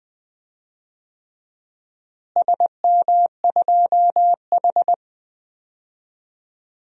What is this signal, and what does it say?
Morse code, heard as a keyed tone, about 21 wpm, 700 Hz. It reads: SM2H
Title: DXCC-100 Call Signs - 4 Characters - Repeated 3x - 20wpm